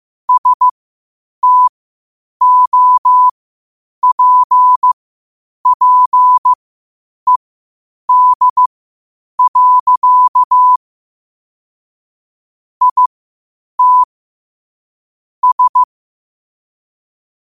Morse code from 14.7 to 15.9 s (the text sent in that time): S